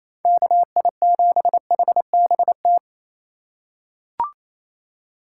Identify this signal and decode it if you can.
Morse code, heard as a keyed tone, about 28 wpm, 700 Hz. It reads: KI7HBT
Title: USA Call Signs - 2x Character Spacing - 28wpm